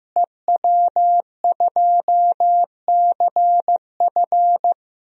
Morse code, heard as a keyed tone, about 15 wpm, 700 Hz. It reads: EW2CF
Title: DXCC-100 Call Signs - 5 Characters - Repeated 3x - 15wpm